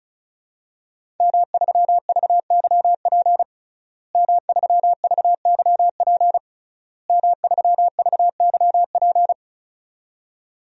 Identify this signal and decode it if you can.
Morse code, heard as a keyed tone, about 34 wpm, 700 Hz. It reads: M3VYP M3VYP M3VYP
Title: DXCC-100 Call Signs - 5 Characters - Repeated 3x - 35wpm